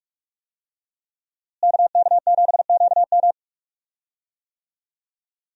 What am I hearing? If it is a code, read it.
Morse code, heard as a keyed tone, about 41 wpm, 700 Hz. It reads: KK7QM
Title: USA Call Signs - 5 Characters - Repeated 3x - 45wpm